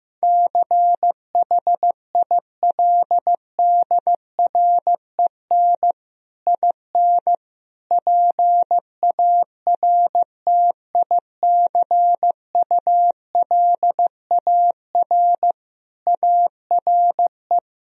Morse code, heard as a keyed tone, about 15 wpm, 700 Hz. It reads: CHILDREN IN PARTICULAR ARE
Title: Sentences from Top 1000 Words - Review Part 4 - 15wpm